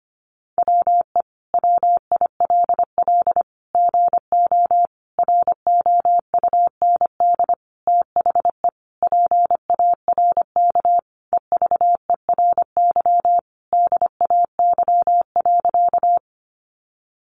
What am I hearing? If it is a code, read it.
Morse code, heard as a keyed tone, about 25 wpm, 700 Hz. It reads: WE WILL GO ROUND THE PARK EVERY DAY.
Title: Sentences from Top 1300 Words 25wpm